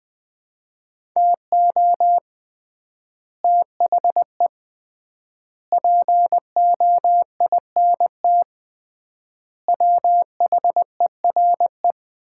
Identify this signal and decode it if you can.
Morse code, heard as a keyed tone, about 20 wpm, 700 Hz. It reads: TO THE POINT WHERE
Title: 3x Word-Spacing - Sets of 4 Words - 20wpm